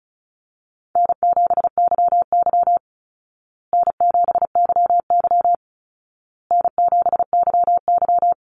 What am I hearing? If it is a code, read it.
Morse code, heard as a keyed tone, about 34 wpm, 700 Hz. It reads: N7YY N7YY N7YY
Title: Easy Call Signs Repeated 3x - 35wpm